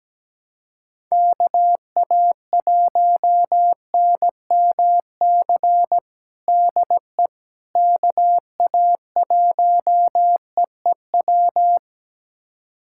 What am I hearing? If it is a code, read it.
Morse code, heard as a keyed tone, about 17 wpm, 700 Hz. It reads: KA1NMC DE KA1EEW
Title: USA Callsign DE Callsign 17wpm